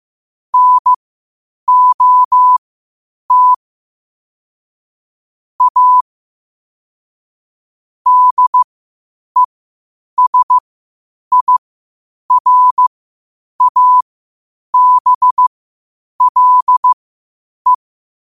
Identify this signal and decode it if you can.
Morse code, heard as a keyed tone, about 15 wpm, spaced as for 8 wpm, 1000 Hz. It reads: NOT A DESIRABLE
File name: qotdfiles / 8wpm